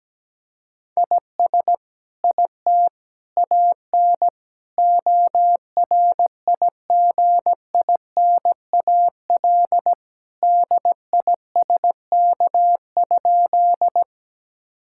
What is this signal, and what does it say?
Morse code, heard as a keyed tone, about 17 wpm, 700 Hz. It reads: IS IT AN ORIGINAL DISK?
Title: Sentences from Top 1000 Words - Review Part 6 - 17wpm